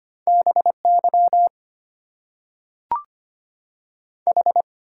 Morse code, heard as a keyed tone, about 25 wpm, 700 Hz. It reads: BY H
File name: Top 90 QSO Elements - 8x Character Spacing - 25wpm